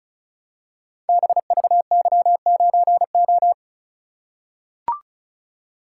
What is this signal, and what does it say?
Morse code, heard as a keyed tone, about 36 wpm, 700 Hz. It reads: BVY9O
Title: Code Groups - 5 Letter:Numbers - 35wpm